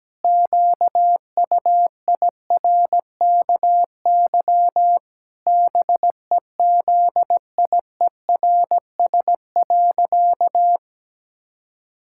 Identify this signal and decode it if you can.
Morse code, heard as a keyed tone, about 17 wpm, 700 Hz. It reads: QUIRKY BEZIERS.
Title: Pangram Challenge v2 - 17wpm